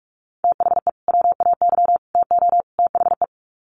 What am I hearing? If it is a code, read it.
Morse code, heard as a keyed tone, about 44 wpm, 700 Hz. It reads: THE WAY TO THE
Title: Sets of 4 Words 45wpm